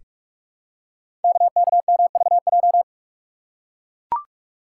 Morse code, heard as a keyed tone, about 44 wpm, 690 Hz. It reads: KKMVJ
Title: Code Groups - 5 Letters - 45wpm